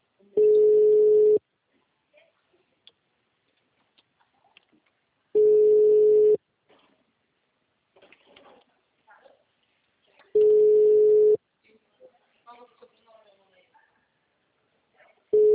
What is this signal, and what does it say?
Ringback tone (Europe, 425 Hz), with speech.
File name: out- -2002- - - .155